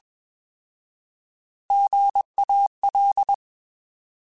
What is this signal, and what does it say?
Morse code, heard as a keyed tone, about 21 wpm, 800 Hz. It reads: GAL
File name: Alphabetical / Alphabetical mp3s / Gs @20